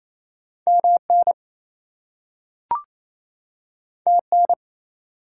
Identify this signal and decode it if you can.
Morse code, heard as a keyed tone, about 28 wpm, 700 Hz. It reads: MN TN
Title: US States and Canadian Provinces 28wpm